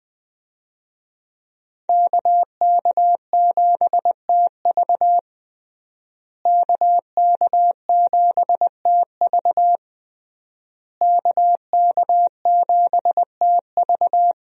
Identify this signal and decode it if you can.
Morse code, heard as a keyed tone, about 21 wpm, 700 Hz. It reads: KK7TV KK7TV KK7TV
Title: USA Call Signs - 5 Characters - Repeated 3x - 20wpm